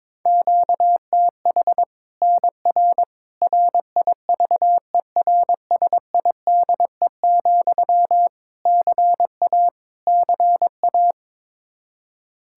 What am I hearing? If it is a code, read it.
Morse code, heard as a keyed tone, about 22 wpm, 700 Hz. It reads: QTH NR RIVERSIDE, CA CA
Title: Practice QSO - Exchange 1 - Beginning - 22wpm